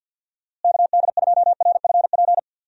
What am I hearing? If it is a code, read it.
Morse code, heard as a keyed tone, about 53 wpm, 700 Hz. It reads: KD2RFP
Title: USA Call Signs - 6 Characters - Repeated 3x - 50wpm